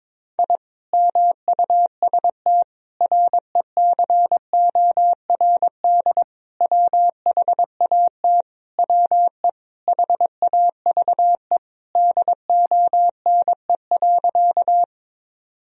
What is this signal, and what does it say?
Morse code, heard as a keyed tone, about 22 wpm, 700 Hz. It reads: I MUST RECORD WHAT WE HAVE DONE.
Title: Sentences from Top 1000 Words - Review Part 3 - 22wpm